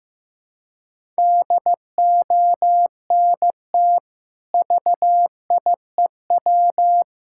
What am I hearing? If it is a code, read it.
Morse code, heard as a keyed tone, about 15 wpm, 700 Hz. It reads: DONT VIEW
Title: Sentences from Top 1000 Words - Review Part 6 - 15wpm